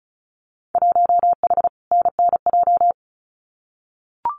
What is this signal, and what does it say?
Morse code, heard as a keyed tone, about 34 wpm, 700 Hz. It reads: 1H NNJ
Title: Winter Field Day Exchanges 35wpm